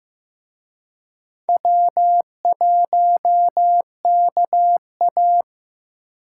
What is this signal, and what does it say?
Morse code, heard as a keyed tone, about 18 wpm, 700 Hz. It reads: W1KA
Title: Speed Racing - USA Call Signs 15wpm